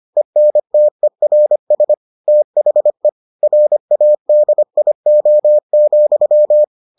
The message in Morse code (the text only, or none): ENTERS THE RADIO,